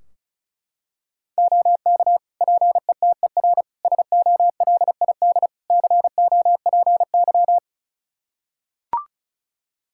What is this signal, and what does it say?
Morse code, heard as a keyed tone, about 34 wpm, 700 Hz. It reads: OK PETER SOLID COPY
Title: K1USN SST QSO Practice 35wpm